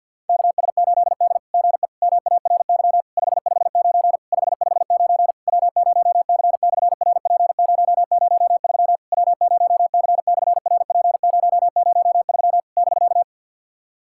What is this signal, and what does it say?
Morse code, heard as a keyed tone, about 54 wpm, 700 Hz. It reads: KS8D GE MARY 559 559 W0C/RP003 W0C/RP003 <BK>
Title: SOTA QSO Practice 50wpm